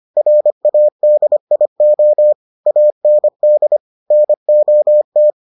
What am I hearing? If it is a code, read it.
Morse code, heard as a keyed tone, about 25 wpm, 600 Hz. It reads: RADIO AND NOT